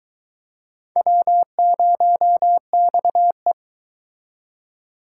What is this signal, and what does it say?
Morse code, heard as a keyed tone, about 24 wpm, 700 Hz. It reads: W0XE